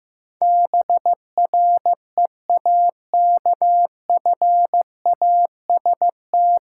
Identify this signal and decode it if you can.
Morse code, heard as a keyed tone, about 15 wpm, 700 Hz. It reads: BREAKFAST
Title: Sentences from Top 1500 Words - New Word Drill Alt - 15wpm